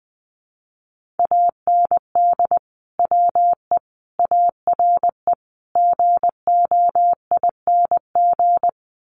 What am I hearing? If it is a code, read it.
Morse code, heard as a keyed tone, about 20 wpm, 700 Hz. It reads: AND WE ARE GOING
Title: Sets of 5 Words Encore2 20wpm